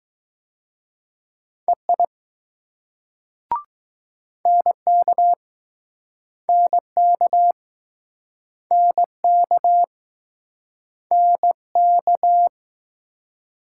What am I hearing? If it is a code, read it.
Morse code, heard as a keyed tone, about 20 wpm, 700 Hz. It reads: EI NK NK NK NK